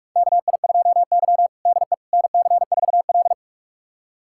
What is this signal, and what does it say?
Morse code, heard as a keyed tone, about 46 wpm, 700 Hz. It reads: KI2Y DE NC4L
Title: USA Callsign DE Callsign - 4 Characters - 45wpm